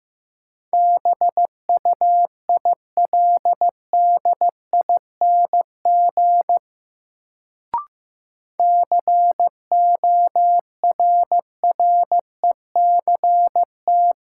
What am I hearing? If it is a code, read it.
Morse code, heard as a keyed tone, about 15 wpm, 700 Hz. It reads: BUILDING CORRECT
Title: Sentences from Top 700 Words - New Word Drill - 15wpm